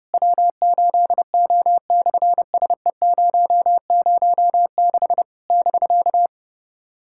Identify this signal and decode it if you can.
Morse code, heard as a keyed tone, about 26 wpm, 700 Hz. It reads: W8O/SE006 <BK>
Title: SOTA QSO Practice 30wpm